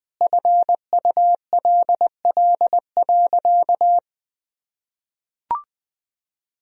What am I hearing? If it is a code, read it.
Morse code, heard as a keyed tone, about 20 wpm, 700 Hz. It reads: FULL.
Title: Sentences from Top 1000 Words - Review Part 3 - 20wpm